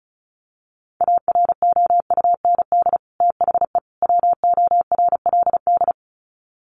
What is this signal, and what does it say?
Morse code, heard as a keyed tone, about 34 wpm, 700 Hz. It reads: AROUND THE WORLD
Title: Sets of 3 Words 35wpm